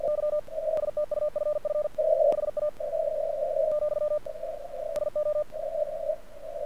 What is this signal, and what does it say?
Morse code, heard as a keyed tone, about 48 wpm, 1200 Hz, with radio noise. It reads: Y STUFF HA , SO
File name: archived / CRSnet